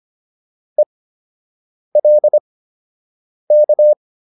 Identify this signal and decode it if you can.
Morse code, heard as a keyed tone, about 25 wpm, spaced as for 7 wpm, 600 Hz. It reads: ELK